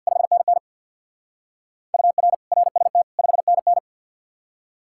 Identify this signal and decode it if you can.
Morse code, heard as a keyed tone, about 59 wpm, 700 Hz. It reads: 5NN UR RST 5NN
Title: Speed-Racing Common QSO Phrases Encore - 50wpm